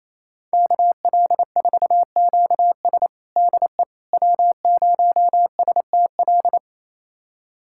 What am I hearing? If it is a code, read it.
Morse code, heard as a keyed tone, about 28 wpm, 700 Hz. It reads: KL4QS DE W0STL
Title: USA Callsign DE Callsign - 5 Characters - 28wpm